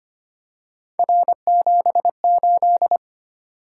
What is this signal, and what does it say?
Morse code, heard as a keyed tone, about 25 wpm, 700 Hz. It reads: R78
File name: Three Letter-Number Combinations - 25wpm